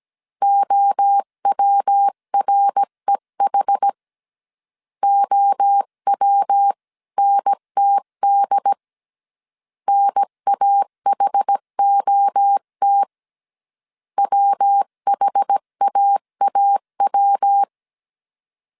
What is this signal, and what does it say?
Morse code, heard as a keyed tone, about 17 wpm, 800 Hz. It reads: OWREH OWNTD NAHOT WHAAW
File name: CW / DET / 1 3 1500C S A 16wpm 2141